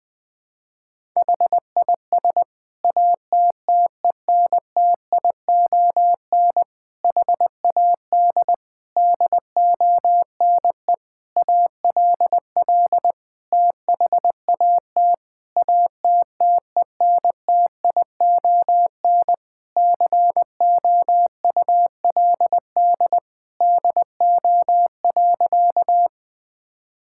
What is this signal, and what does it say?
Morse code, heard as a keyed tone, about 21 wpm, 700 Hz. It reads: HIS ATTENTION HAD DONE ALL THAT ATTENTION COULD DO.
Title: Sentences from Top 1000 Words - Review Part 6 - 20wpm